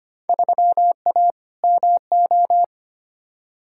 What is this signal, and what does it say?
Morse code, heard as a keyed tone, about 25 wpm, 700 Hz. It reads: 3A MO
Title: Field Day Exchanges Encore - 25wpm